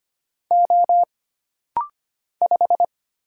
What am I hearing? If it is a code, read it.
Morse code, heard as a keyed tone, about 25 wpm, 700 Hz. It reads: O 5
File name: Single Character (Letter:Number:Punctuation:Prosign) - Rapid-Fire - 25wpm